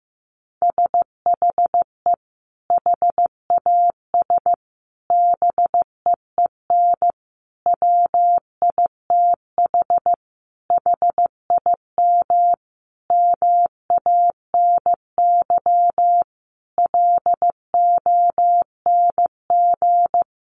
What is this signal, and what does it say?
Morse code, heard as a keyed tone, about 15 wpm, 700 Hz. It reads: SHE HAS BEEN WITH HIM MANY LONG
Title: Sentences from Top 200 Words 15wpm